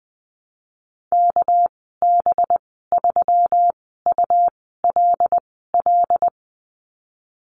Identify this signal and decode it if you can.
Morse code, heard as a keyed tone, about 20 wpm, 700 Hz. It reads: K B 3 U L L
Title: 238 Lesson 38g - Letter q - Call Signs 20wpm